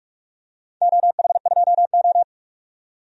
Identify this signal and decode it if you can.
Morse code, heard as a keyed tone, about 42 wpm, 700 Hz. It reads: OH2O
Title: DXCC-100 Call Signs - 4 Characters - 45wpm